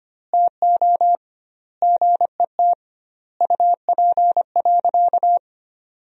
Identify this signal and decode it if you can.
Morse code, heard as a keyed tone, about 25 wpm, 700 Hz. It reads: TO GET UP.
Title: 2x Word-Spacing - Sentences from Top 500 Words - 25wpm